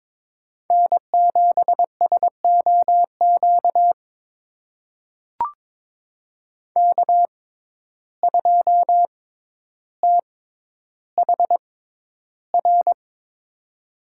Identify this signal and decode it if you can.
Morse code, heard as a keyed tone, about 22 wpm, spaced as for 11 wpm, 700 Hz. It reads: N7SOQ K2THR
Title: USA Call Signs - 5 Characters - 6x Character Spacing - 22wpm